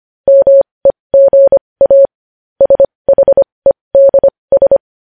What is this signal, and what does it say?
Morse code, heard as a keyed tone, about 25 wpm, spaced as for 20 wpm, 550 Hz. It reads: MEGA SHEDS
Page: News Headlines in Morse Code at 20 WPM